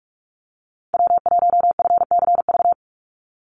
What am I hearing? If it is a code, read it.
Morse code, heard as a keyed tone, about 44 wpm, 700 Hz. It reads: W1FCV
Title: Call Signs - 45wpm